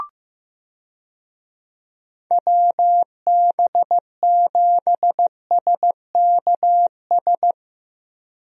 Morse code, heard as a keyed tone, about 15 wpm, 700 Hz. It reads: WB7SKS
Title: USA Call Signs - 6 Characters - Repeated 3x - 15wpm